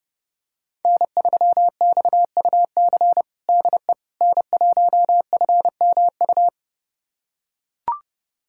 Morse code, heard as a keyed tone, about 32 wpm, 700 Hz. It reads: N3XUC DE N1FMU
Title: USA Callsign DE Callsign - 5 Characters - 30wpm